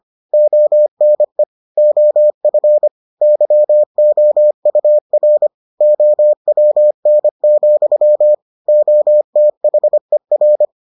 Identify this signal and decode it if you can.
Morse code, heard as a keyed tone, about 25 wpm, 600 Hz. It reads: ONE OF YOUR OWN, OTHER